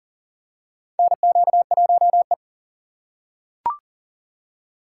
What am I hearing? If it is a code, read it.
Morse code, heard as a keyed tone, about 40 wpm, 700 Hz. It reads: NQ1E
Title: USA Call Signs - 3x Character Spacing - 40wpm